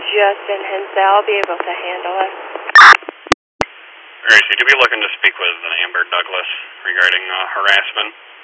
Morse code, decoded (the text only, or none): none